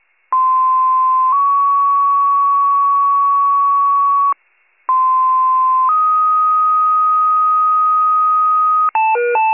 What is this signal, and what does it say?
Two-tone page, heard as a one-second tone, then a three-second tone, about 1040 then 1120 Hz, about 1040 then 1250 Hz. It is heard from 0.3 to 8.9 s.